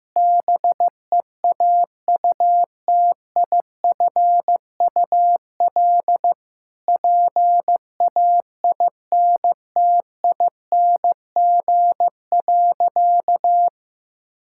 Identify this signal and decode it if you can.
Morse code, heard as a keyed tone, about 15 wpm, 700 Hz. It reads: BEAUTIFUL PAINTING.
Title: Sentences from Top 1000 Words - Review Part 6 - 15wpm